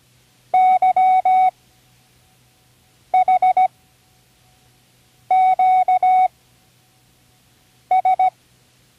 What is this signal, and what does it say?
Morse code, heard as a keyed tone, about 17 wpm, 750 Hz. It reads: Y H Q S
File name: CWResources / Learning Morse Code (from Tune in the World) / 24 random